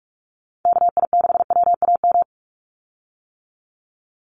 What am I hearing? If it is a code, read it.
Morse code, heard as a keyed tone, about 44 wpm, 700 Hz. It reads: KI6WAM
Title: USA Call Signs 45wpm